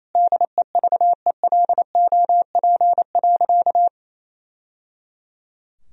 Morse code, heard as a keyed tone, about 28 wpm, 700 Hz. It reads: DEVELOP.
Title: Sentences from Top 1000 Words - Review Part 2 - 28wpm